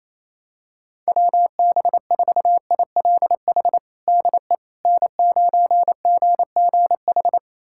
Morse code, heard as a keyed tone, about 28 wpm, 700 Hz. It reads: WB4ILH DE N9GGH
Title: USA Callsign DE Callsign 28wpm